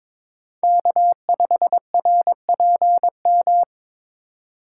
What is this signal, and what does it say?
Morse code, heard as a keyed tone, about 22 wpm, 700 Hz. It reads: K5RPM
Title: USA Call Signs - 3x Character Spacing - 22wpm